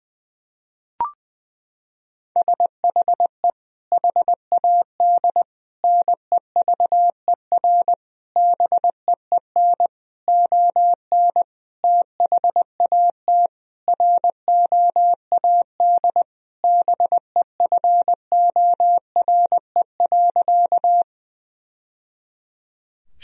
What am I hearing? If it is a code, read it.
Morse code, heard as a keyed tone, about 20 wpm, 700 Hz. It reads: SHE HAD NEVER BEEN ON THAT ROAD BEFORE.
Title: Sentences from Top 1000 Words - Review Part 2 - 20wpm